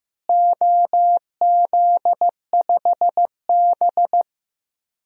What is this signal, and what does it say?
Morse code, heard as a keyed tone, about 15 wpm, 700 Hz. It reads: OZ5B
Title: DXCC-100 Call Signs - 4 Characters - Repeated 3x - 15wpm